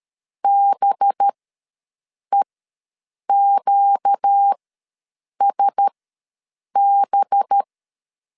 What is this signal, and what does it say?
Morse code, heard as a keyed tone, about 13 wpm, spaced as for 7 wpm, 800 Hz. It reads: BEQSB
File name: LEZIONI / 5 LBVC / 5 LB ws 12wpm 3034